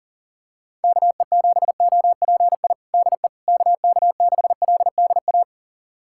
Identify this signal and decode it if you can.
Morse code, heard as a keyed tone, about 40 wpm, 700 Hz. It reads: KE7OPI DE KK6LDA
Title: USA Callsign DE Callsign - 6 Characters - 40wpm